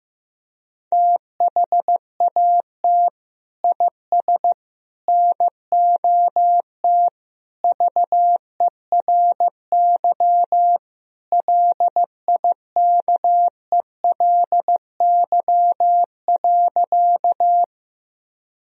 Morse code, heard as a keyed tone, about 15 wpm, 700 Hz. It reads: THAT IS NOT VERY LIKELY.
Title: Sentences from Top 1000 Words - Review Part 4 - 15wpm